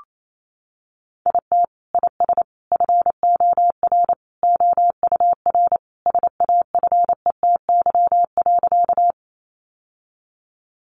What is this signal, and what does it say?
Morse code, heard as a keyed tone, about 28 wpm, 700 Hz. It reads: IT IS FOR OUR SAFETY.
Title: Sentences from Top 700 Words 28wpm